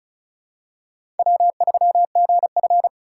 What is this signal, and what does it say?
Morse code, heard as a keyed tone, about 36 wpm, 700 Hz. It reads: W3GF
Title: USA Call Signs - 4 Characters - 2x Character Spacing - 35wpm